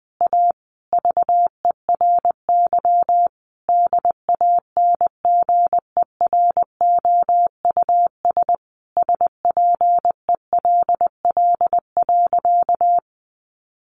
Morse code, heard as a keyed tone, about 20 wpm, 700 Hz. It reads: A VERY DANGEROUS SPELL.
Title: Sentences from Top 1200 Words 20wpm